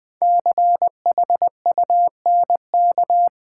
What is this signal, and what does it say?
Morse code, heard as a keyed tone, about 20 wpm, 700 Hz. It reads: CHUNK
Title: 5 Letter Words - First 3 Letters Given - 20wpm